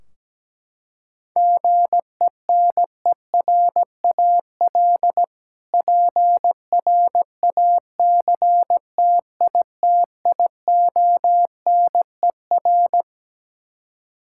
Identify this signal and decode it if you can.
Morse code, heard as a keyed tone, about 17 wpm, 700 Hz. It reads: GENERAL PRACTITIONER
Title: K1USN SST QSO Practice 17wpm